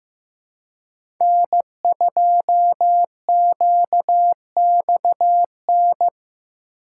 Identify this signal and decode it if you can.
Morse code, heard as a keyed tone, about 15 wpm, 700 Hz. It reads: N2QXN
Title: USA Call Signs - 5 Characters - 6x Character Spacing - 15wpm